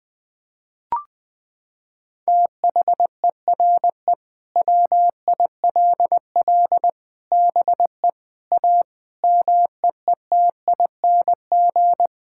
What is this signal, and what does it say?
Morse code, heard as a keyed tone, about 21 wpm, 700 Hz. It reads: THERE WILL BE A MEETING
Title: Sentences with 4-Word Phrases - 20wpm